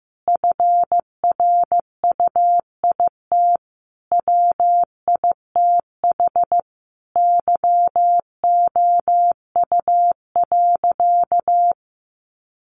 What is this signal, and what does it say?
Morse code, heard as a keyed tone, about 15 wpm, 700 Hz. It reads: FRUIT WITH YOU.
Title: Sentences from Top 1300 Words 15wpm